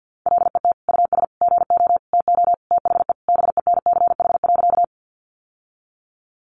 Morse code, heard as a keyed tone, about 50 wpm, 700 Hz. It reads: LET US GO TO THE BENCH.
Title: Sentences from Top 1400 Words 50wpm